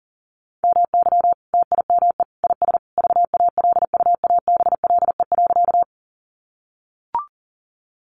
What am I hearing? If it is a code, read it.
Morse code, heard as a keyed tone, about 38 wpm, 700 Hz. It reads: MY TIME IS VALUABLE.
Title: Sentences from Top 1100 Words 40wpm